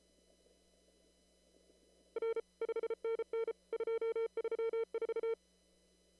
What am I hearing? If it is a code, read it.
Morse code, heard as a keyed tone, about 33 wpm, 450 Hz. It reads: R 5NN 234
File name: CW / kt5j wpx cw-op-k5pi